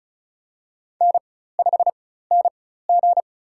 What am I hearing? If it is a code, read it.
Morse code, heard as a keyed tone, about 36 wpm, spaced as for 16 wpm, 700 Hz. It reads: N5NG
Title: USA Call Signs - 4 Characters - 4x Character Spacing - 35wpm